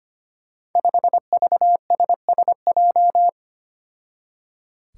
Morse code, heard as a keyed tone, about 25 wpm, 700 Hz. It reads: 5VSSJ
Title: Five Challenging Letter-Number Combinations 25wpm